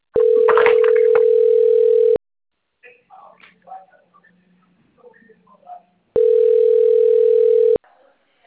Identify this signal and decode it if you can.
Ringback tone (North America), 440 + 480 Hz.
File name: rg-1600-unknown- - - .18614